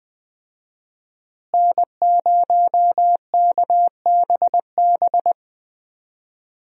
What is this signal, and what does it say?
Morse code, heard as a keyed tone, about 20 wpm, 700 Hz. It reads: N0KBB
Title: USA Call Signs - 5 Characters - Repeated 2x - 20wpm